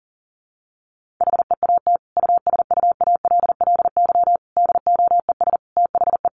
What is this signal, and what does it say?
Morse code, heard as a keyed tone, about 40 wpm, 700 Hz. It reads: HEAT USUALLY DOES THE
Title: Sentences from Top 1500 Words 40wpm